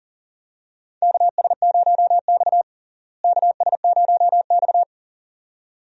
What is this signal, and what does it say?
Morse code, heard as a keyed tone, about 40 wpm, 700 Hz. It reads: KS0X KS0X
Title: USA Call Signs - 4 Characters - Repeated 2x - 40wpm